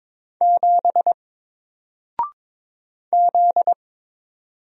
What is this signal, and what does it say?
Morse code, heard as a keyed tone, about 22 wpm, 700 Hz. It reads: 7 Z
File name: Single-Letter-Number-22wpm